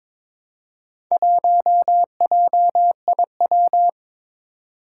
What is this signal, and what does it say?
Morse code, heard as a keyed tone, about 22 wpm, 700 Hz. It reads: 1JIW
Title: Four Challenging Letter-Number Combinations 22wpm